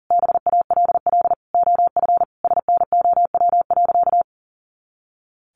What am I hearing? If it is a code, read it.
Morse code, heard as a keyed tone, about 40 wpm, 700 Hz. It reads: BALL OF SNOW.
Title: Easy Phrases 40wpm